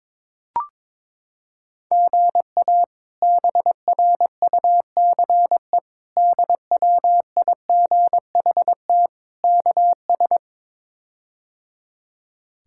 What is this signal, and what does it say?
Morse code, heard as a keyed tone, about 22 wpm, 700 Hz. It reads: GA BRUCE DWIGHT KS
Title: K1USN SST QSO Practice - 22wpm